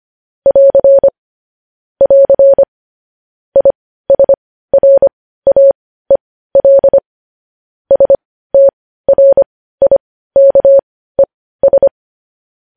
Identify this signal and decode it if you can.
Morse code, heard as a keyed tone, about 25 wpm, spaced as for 15 wpm, 550 Hz. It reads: <AR> <AR> ISRAEL STRIKES
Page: news headlines in morse code at 15 wpm